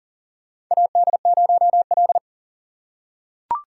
Morse code, heard as a keyed tone, about 40 wpm, 700 Hz. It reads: AD0L
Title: USA Call Signs - 4 Characters - 4x Character Spacing - 40wpm